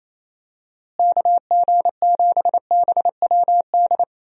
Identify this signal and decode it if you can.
Morse code, heard as a keyed tone, about 27 wpm, 700 Hz. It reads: KG7BWD
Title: USA Call Signs - 6 Characters - Repeated 3x - 28wpm